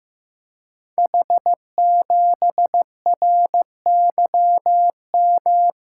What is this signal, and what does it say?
Morse code, heard as a keyed tone, about 15 wpm, 700 Hz. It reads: H7RYM
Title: Code Groups - 5 Letter:Numbers - 15wpm